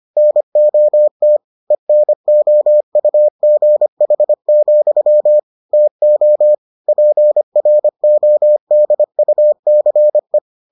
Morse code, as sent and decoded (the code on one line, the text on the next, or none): -. --- - / . -. --- ..- --. .... --..-- / - --- / .--. .-. --- -.. ..- -.-. .
NOT ENOUGH, TO PRODUCE